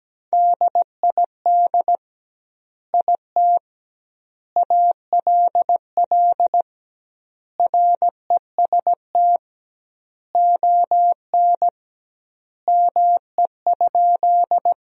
Morse code, heard as a keyed tone, about 17 wpm, 700 Hz. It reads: DID IT ALL REST ON ME?
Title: 2x Word-Spacing - Sentences from Top 500 Words - 17wpm